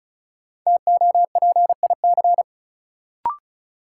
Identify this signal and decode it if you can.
Morse code, heard as a keyed tone, about 36 wpm, 700 Hz. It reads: TOPIC
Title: Sentences from Top 1200 Words - New Word Drill Alt - 35wpm